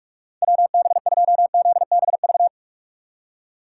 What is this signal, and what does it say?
Morse code, heard as a keyed tone, about 44 wpm, 700 Hz. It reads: WB2ZBV
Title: USA Call Signs - 6x Character Spacing - 45wpm